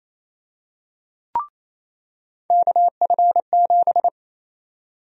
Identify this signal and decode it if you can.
Morse code, heard as a keyed tone, about 28 wpm, 700 Hz. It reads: KF7
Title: USA Call Sign Prefixes - 28wpm